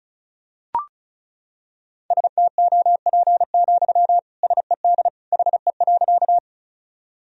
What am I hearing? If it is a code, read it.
Morse code, heard as a keyed tone, about 36 wpm, 700 Hz. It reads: STOP, SED HE.
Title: Sentences from Top 300 Words and Abbreviations 35wpm